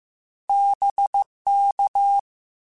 Morse code, heard as a keyed tone, about 15 wpm, 800 Hz. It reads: BK
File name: Misc / Misc-mp3s / Words @13